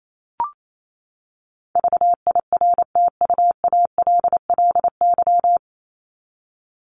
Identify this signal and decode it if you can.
Morse code, heard as a keyed tone, about 28 wpm, 700 Hz. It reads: VIRTUALLY
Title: LY Suffix - Top 100 Words with Base Words 28wpm